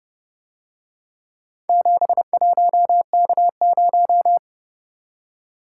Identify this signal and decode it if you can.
Morse code, heard as a keyed tone, about 30 wpm, 700 Hz. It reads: 71K0